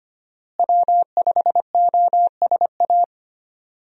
Morse code, heard as a keyed tone, about 25 wpm, 700 Hz. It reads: W5OSA
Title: USA Call Signs - 5 Characters - Repeated 2x - 25wpm